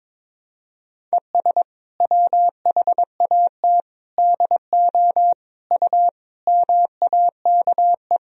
Morse code, heard as a keyed tone, about 22 wpm, 700 Hz. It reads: ES WHAT DO U MAKE
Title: Sentences from Top 100 Words and Abbreviations 22wpm